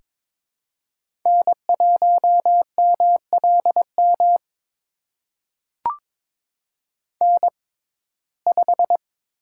Morse code, heard as a keyed tone, about 22 wpm, spaced as for 16 wpm, 700 Hz. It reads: N1MLM N5
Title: USA Call Signs - 6x Character Spacing - 22wpm